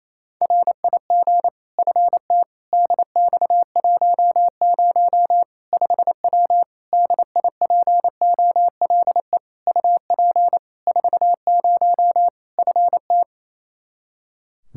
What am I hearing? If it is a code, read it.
Morse code, heard as a keyed tone, about 28 wpm, 700 Hz. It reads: RIG FT DX10 5W DIPOLE UP 40 FT
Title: Practice QSO - Exchange 2 - Info - 28wpm